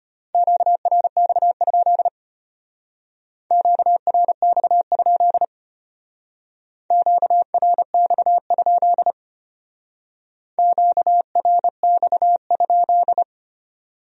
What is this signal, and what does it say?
Morse code, heard as a keyed tone, about 32 wpm, 700 Hz. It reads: QRX? QRX? QRX? QRX?